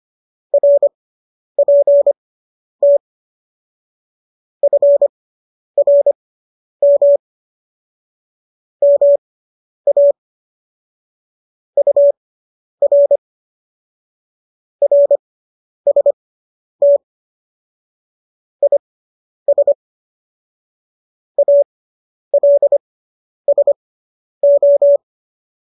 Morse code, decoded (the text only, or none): RPT FRM MA UR RST IS ALSO